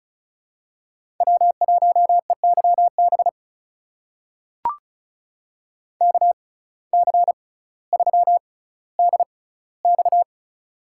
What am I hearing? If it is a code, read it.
Morse code, heard as a keyed tone, about 34 wpm, spaced as for 18 wpm, 700 Hz. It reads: W1EYB KC3DX
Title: USA Call Signs - 5 Characters - 6x Character Spacing - 35wpm